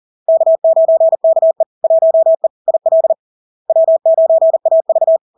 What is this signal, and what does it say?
Morse code, heard as a keyed tone, about 40 wpm, 650 Hz. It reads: K9KE 1E IL W9AV
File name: LARK-FD-2022-Log-in-CW-40wpm-650-Hz